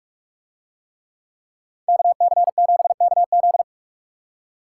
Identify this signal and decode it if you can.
Morse code, heard as a keyed tone, about 45 wpm, 700 Hz. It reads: KC7KZ